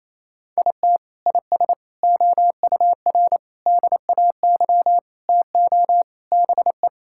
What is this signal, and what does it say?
Morse code, heard as a keyed tone, about 28 wpm, 700 Hz. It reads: IT IS OUR DAY TO BE
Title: Sentences from Top 1000 Words - Review Part 5 - 28wpm